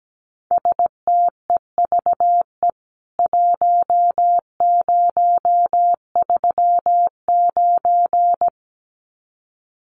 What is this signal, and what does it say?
Morse code, heard as a keyed tone, about 17 wpm, 700 Hz. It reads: STEVE 1039
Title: CWT Exchange 17wpm